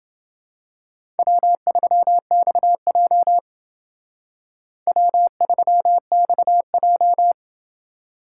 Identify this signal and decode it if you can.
Morse code, heard as a keyed tone, about 28 wpm, 700 Hz. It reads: W3XJ W3XJ